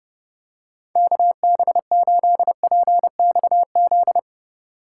Morse code, heard as a keyed tone, about 29 wpm, 700 Hz. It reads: KB8PXZ